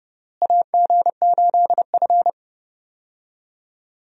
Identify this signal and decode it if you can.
Morse code, heard as a keyed tone, about 30 wpm, 700 Hz. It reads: AG8F
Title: USA Call Signs - 4 Characters - Repeated 3x - 30wpm